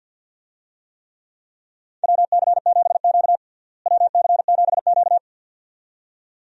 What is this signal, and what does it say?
Morse code, heard as a keyed tone, about 52 wpm, 700 Hz. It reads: WC7Q WC7Q
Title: USA Call Signs - 4 Characters - Repeated 2x - 50wpm